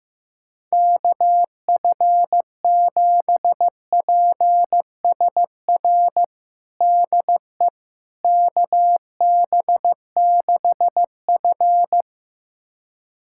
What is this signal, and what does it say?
Morse code, heard as a keyed tone, about 15 wpm, 700 Hz. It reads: KF7PSR DE KB6F
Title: USA Callsign DE Callsign 15wpm